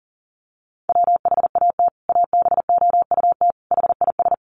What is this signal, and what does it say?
Morse code, heard as a keyed tone, about 40 wpm, 700 Hz. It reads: WHAT ABOUT HIS
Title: Sentences from Top 1100 Words 40wpm